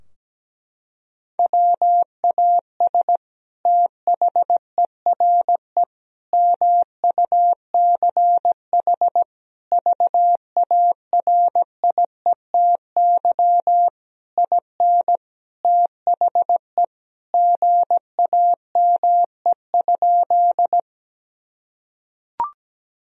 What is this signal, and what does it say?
Morse code, heard as a keyed tone, about 17 wpm, 700 Hz. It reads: WAS THERE MUCH VARIETY IN THE GAME?
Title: Sentences from Top 1000 Words - Review Part 4 - 17wpm